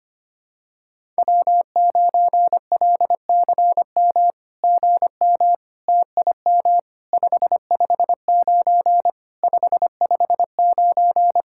Morse code, heard as a keyed tone, about 24 wpm, 700 Hz. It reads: W9LCM GM TIM 559 559
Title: SOTA QSO Practice 25wpm